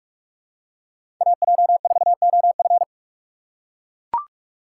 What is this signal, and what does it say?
Morse code, heard as a keyed tone, about 44 wpm, 700 Hz. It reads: AJ4OF
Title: USA Call Signs - 5 Characters - 6x Character Spacing - 45wpm